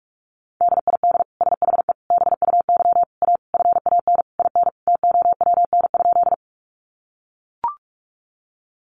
Morse code, heard as a keyed tone, about 44 wpm, 700 Hz. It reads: DID SHE BUY A FAN IN TOWN?
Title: Sentences from Top 1300 Words 45wpm